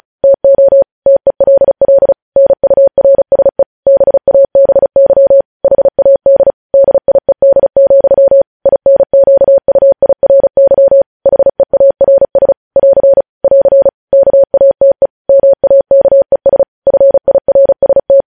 Morse code, heard as a keyed tone, about 36 wpm, 550 Hz. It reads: TO TELL NURSE BABY HAD DIED, INQUIRY HEARS <AR> <AR> KATE MAKES FIRST